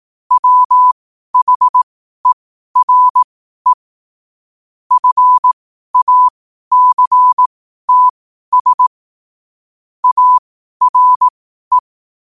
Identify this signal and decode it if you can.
Morse code, heard as a keyed tone, about 18 wpm, spaced as for 12 wpm, 1000 Hz. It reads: WHERE FACTS ARE
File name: qotdfiles / 12wpm